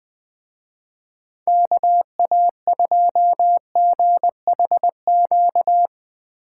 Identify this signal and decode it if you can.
Morse code, heard as a keyed tone, about 20 wpm, 700 Hz. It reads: KA2GHQ